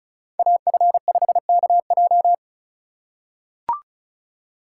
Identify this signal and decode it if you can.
Morse code, heard as a keyed tone, about 34 wpm, 700 Hz. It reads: AF5KJ
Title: USA Call Signs - 5 Characters - 4x Character Spacing - 35wpm